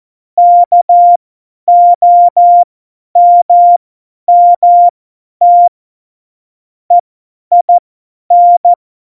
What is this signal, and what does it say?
Morse code, heard as a keyed tone, about 14 wpm, spaced as for 10 wpm, 700 Hz. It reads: KOMMT EIN
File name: Morsekurs / Witze / Witz 30. Oktober 2018, 50 F70